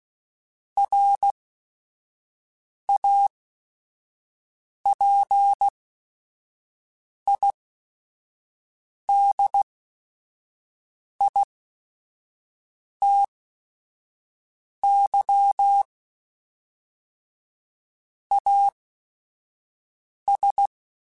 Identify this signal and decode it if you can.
Morse code, heard as a keyed tone, about 16 wpm, spaced as for 5 wpm, 800 Hz. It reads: RAPIDITY AS
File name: Technical Subjects / Technical Subjects-mp3s / XL @5